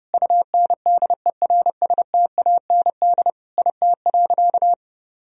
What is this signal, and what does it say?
Morse code, heard as a keyed tone, about 32 wpm, 700 Hz. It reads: UNDERSTAND IT.
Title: Sentences from Top 1000 Words - Review Part 4 - 30wpm